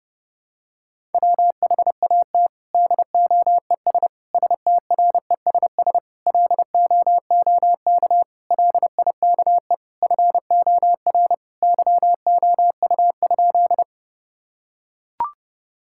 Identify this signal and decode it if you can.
Morse code, heard as a keyed tone, about 32 wpm, 700 Hz. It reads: WHAT DOES STRESS LOOK LIKE FOR YOU?
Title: Sentences from Top 1000 Words - Review Part 6 - 30wpm